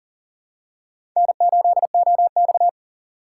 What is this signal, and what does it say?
Morse code, heard as a keyed tone, about 40 wpm, 700 Hz. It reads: N8OX
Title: USA Call Signs - 4 Characters - 3x Character Spacing - 40wpm